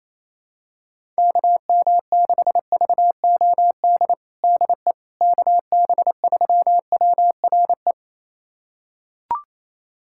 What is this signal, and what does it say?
Morse code, heard as a keyed tone, about 28 wpm, 700 Hz. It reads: KM6VOD DE KB3WRE
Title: USA Callsign DE Callsign 28wpm